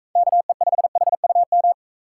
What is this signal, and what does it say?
Morse code, heard as a keyed tone, about 41 wpm, 700 Hz. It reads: KE5HUM